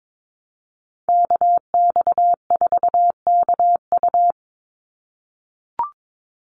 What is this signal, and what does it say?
Morse code, heard as a keyed tone, about 22 wpm, 700 Hz. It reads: KX4KU
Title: Call Signs - 22wpm